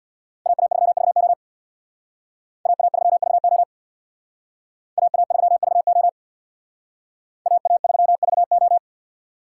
Morse code, heard as a keyed tone, about 65 wpm, 700 Hz. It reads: AA3VO AA3VO AA3VO AA3VO